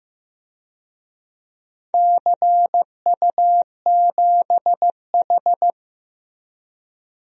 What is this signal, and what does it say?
Morse code, heard as a keyed tone, about 15 wpm, 700 Hz. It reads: CU7H
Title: DXCC-100 Call Signs - 4 Characters - Repeated 3x - 15wpm